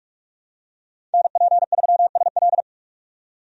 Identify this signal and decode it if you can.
Morse code, heard as a keyed tone, about 46 wpm, 700 Hz. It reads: NP3SL
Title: USA Call Signs - 5 Characters - 45wpm